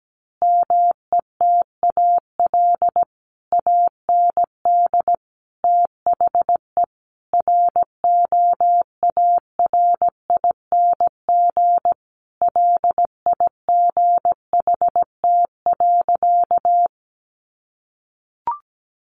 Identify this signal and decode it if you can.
Morse code, heard as a keyed tone, about 17 wpm, 700 Hz. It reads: METAL AND THE ROARING LIGHT.
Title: A Sound Of Thunder 17wpm